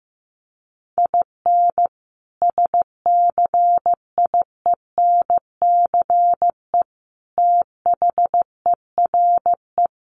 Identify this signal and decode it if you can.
Morse code, heard as a keyed tone, about 15 wpm, 700 Hz. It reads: IN SCIENCE THERE
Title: Sentences from Top 1400 Words 15wpm